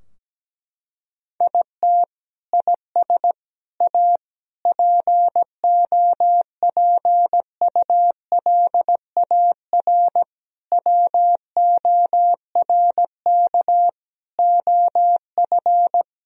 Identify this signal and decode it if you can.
Morse code, heard as a keyed tone, about 17 wpm, 700 Hz. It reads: IT IS A POPULAR WORK OF
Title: Sentences from Top 1000 Words - Review Part 3 - 17wpm